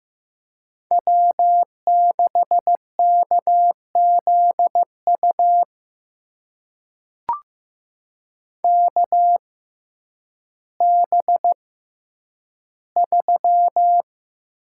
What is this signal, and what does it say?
Morse code, heard as a keyed tone, about 15 wpm, spaced as for 9 wpm, 700 Hz. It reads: W6KZU KB3
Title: USA Call Signs - 6x Character Spacing - 15wpm